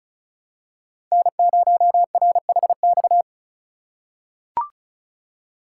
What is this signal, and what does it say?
Morse code, heard as a keyed tone, about 36 wpm, 700 Hz. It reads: N0RHX
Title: USA Call Signs - 5 Characters - 4x Character Spacing - 35wpm